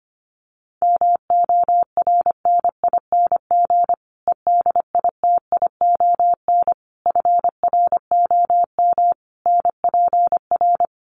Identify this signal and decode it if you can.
Morse code, heard as a keyed tone, about 25 wpm, 700 Hz. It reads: MORNING EDITION FROM NPR
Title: Sets of 5 Words Encore 25wpm